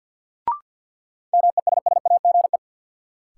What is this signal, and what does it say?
Morse code, heard as a keyed tone, about 53 wpm, 700 Hz. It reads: MESSAGE
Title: Sentences from Top 1100 Words - New Word Drill Alt - 50wpm